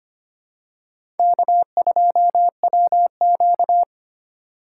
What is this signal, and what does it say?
Morse code, heard as a keyed tone, about 25 wpm, 700 Hz. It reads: K2WQ
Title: USA Call Signs - 4 Characters - 3x Character Spacing - 25wpm